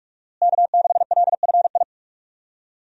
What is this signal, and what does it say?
Morse code, heard as a keyed tone, about 41 wpm, 700 Hz. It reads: K6LFI
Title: USA Call Signs - 5 Characters - Repeated 3x - 45wpm